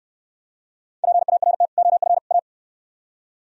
Morse code, heard as a keyed tone, about 67 wpm, 700 Hz. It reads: PART OF A